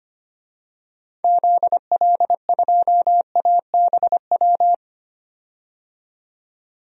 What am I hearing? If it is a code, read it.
Morse code, heard as a keyed tone, about 25 wpm, 700 Hz. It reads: ZL2ABW
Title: DXCC-100 Call Signs - 6 Characters - 25wpm